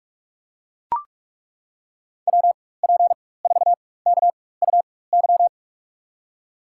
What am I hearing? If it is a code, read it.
Morse code, heard as a keyed tone, about 41 wpm, spaced as for 20 wpm, 700 Hz. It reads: WP4KUY
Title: USA Call Signs - 6 Characters - 4x Character Spacing - 45wpm